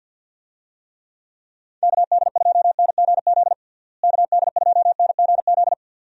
Morse code, heard as a keyed tone, about 53 wpm, 700 Hz. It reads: KD2NGZ KD2NGZ
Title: USA Call Signs - 6 Characters - Repeated 2x - 50wpm